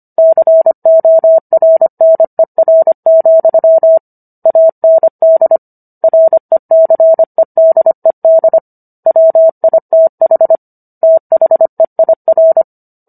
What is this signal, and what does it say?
Morse code, heard as a keyed tone, about 25 wpm, 650 Hz. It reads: CORNER, AND RECEDED WITH THEIR